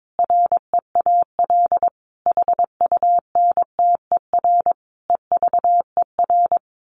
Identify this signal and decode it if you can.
Morse code, heard as a keyed tone, about 22 wpm, 700 Hz. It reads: REAL HUNTER EVER
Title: A Sound Of Thunder 22wpm